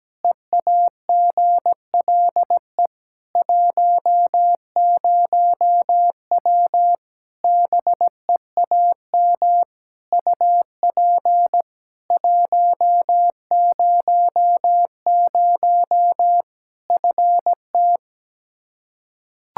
Morse code, heard as a keyed tone, about 17 wpm, 700 Hz. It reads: EAGLE 10W BEAM UP 100 FT
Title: Practice QSO - Exchange 2 - Info - 17wpm